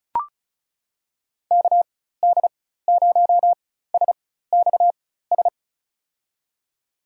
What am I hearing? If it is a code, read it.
Morse code, heard as a keyed tone, about 36 wpm, spaced as for 16 wpm, 700 Hz. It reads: KD0SXS
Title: USA Call Signs - 6 Characters - 4x Character Spacing - 35wpm